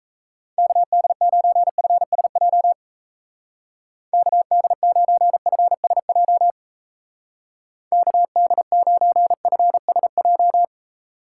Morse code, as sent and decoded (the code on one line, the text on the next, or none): -.- -.. ----. ..-. ... .--- / -.- -.. ----. ..-. ... .--- / -.- -.. ----. ..-. ... .---
KD9FSJ KD9FSJ KD9FSJ